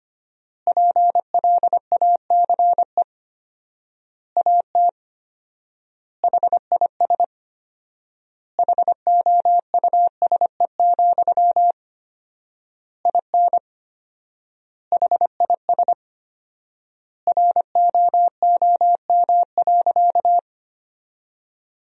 Morse code, heard as a keyed tone, about 25 wpm, 700 Hz. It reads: PLACE AT HIS HOUSE, IN HIS ROOM.
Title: 4x Word-Spacing - Sentences from Top 300 Words - 25wpm